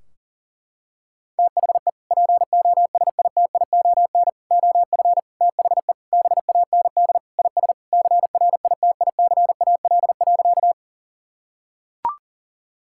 Morse code, heard as a keyed tone, about 40 wpm, 700 Hz. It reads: THE POSITION OF THE BAND IS CRITICAL.
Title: Sentences from Top 1600 Words 40wpm